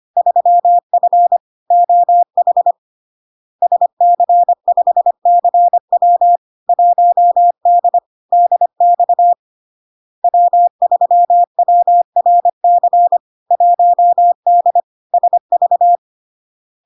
Morse code, heard as a keyed tone, about 25 wpm, 700 Hz. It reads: 3F OH SC5CW 1D DX W3WRC 1D SV
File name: LARK-FD-2022-Log-in-CW-25wpm-700-Hz